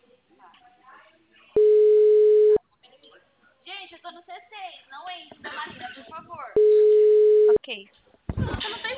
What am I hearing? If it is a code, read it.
Ringback tone (Europe, 425 Hz), with speech.